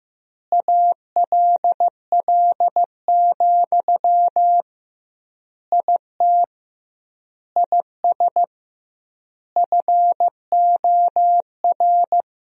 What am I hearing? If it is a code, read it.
Morse code, heard as a keyed tone, about 15 wpm, 700 Hz. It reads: ALL, IT IS FOR
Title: 2x Word-Spacing - Sentences from Top 500 Words - 15wpm